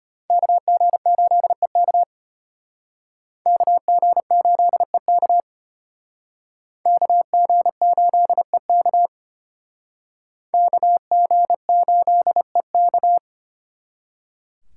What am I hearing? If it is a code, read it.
Morse code, heard as a keyed tone, about 33 wpm, 700 Hz. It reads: KG8EK KG8EK KG8EK KG8EK